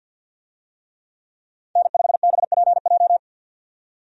Morse code, heard as a keyed tone, about 50 wpm, 700 Hz. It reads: N5BPJ
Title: USA Call Signs - 5 Characters - 50wpm